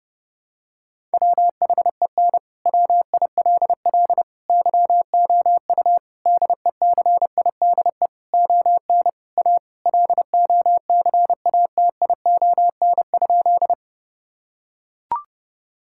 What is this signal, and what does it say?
Morse code, heard as a keyed tone, about 30 wpm, 700 Hz. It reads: WHEN WILL YOU DECIDE ON A LOCATION?
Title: Sentences from Top 1000 Words - Review Part 6 - 30wpm